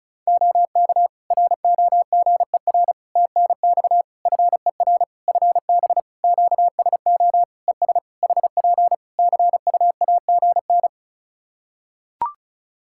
Morse code, heard as a keyed tone, about 36 wpm, 700 Hz. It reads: OK ROGER TNX FER FB QSO ES HP CUAGN
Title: Practice QSO - Exchange 4 - Ending - 35wpm